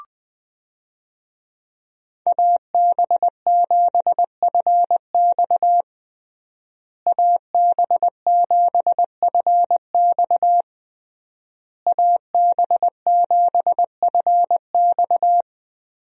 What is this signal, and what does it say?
Morse code, heard as a keyed tone, about 20 wpm, 700 Hz. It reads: AB7FX AB7FX AB7FX
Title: USA Call Signs - 5 Characters - Repeated 3x - 20wpm